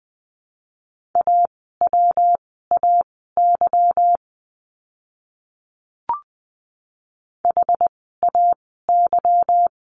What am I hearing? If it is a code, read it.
Morse code, heard as a keyed tone, about 20 wpm, spaced as for 14 wpm, 700 Hz. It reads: AWAY HAY
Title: 146 Lesson 24f - Letter y - Words 20wpm